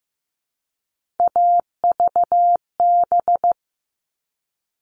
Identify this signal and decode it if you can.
Morse code, heard as a keyed tone, about 15 wpm, 700 Hz. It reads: AVB
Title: All Cut Numbers - 3-Digit 15wpm